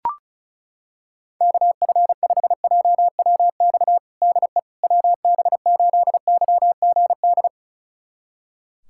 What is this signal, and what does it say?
Morse code, heard as a keyed tone, about 36 wpm, 700 Hz. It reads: KF5JWX DE WB8YGD
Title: USA Callsign DE Callsign 35wpm